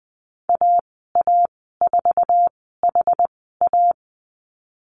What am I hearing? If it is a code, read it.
Morse code, heard as a keyed tone, about 20 wpm, spaced as for 15 wpm, 700 Hz. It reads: AA4HA
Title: 069 Lesson 12g - Letter d - Call Signs 20wpm